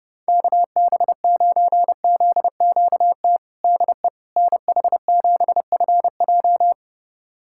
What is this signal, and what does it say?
Morse code, heard as a keyed tone, about 32 wpm, 700 Hz. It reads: KB9ZQT DE NH7FJ
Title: USA Callsign DE Callsign 30wpm